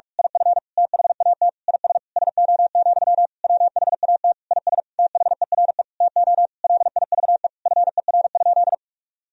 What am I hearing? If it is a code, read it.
Morse code, heard as a keyed tone, about 47 wpm, 700 Hz. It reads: IF THAT IS SO, WHAT IS THERE TO LIVE FER?
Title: Sentences from Top 200 Words and Abbreviations 45wpm